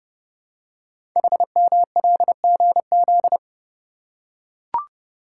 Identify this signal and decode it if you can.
Morse code, heard as a keyed tone, about 28 wpm, 700 Hz. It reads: HMLGZ
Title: Code Groups - 5 Letters - 30wpm